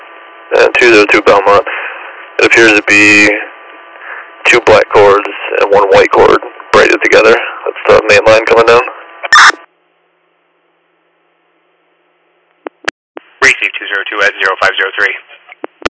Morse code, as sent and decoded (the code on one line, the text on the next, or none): none